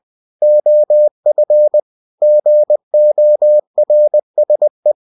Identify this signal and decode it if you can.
Morse code, heard as a keyed tone, about 20 wpm, 600 Hz. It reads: OF GORSE
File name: cw-books / winnie-the-pooh / winnie-20wpm-page-0028